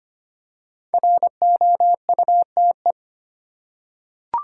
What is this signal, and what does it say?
Morse code, heard as a keyed tone, about 25 wpm, 700 Hz. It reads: ROUTE
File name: 5 Letter Words - First 3 Letters Given - 25wpm